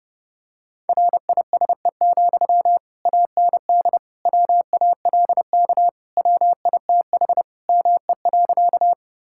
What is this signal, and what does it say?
Morse code, heard as a keyed tone, about 30 wpm, 700 Hz. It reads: RISE, AND WALK WITH ME.
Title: Sentences from Top 1000 Words - Review Part 6 - 30wpm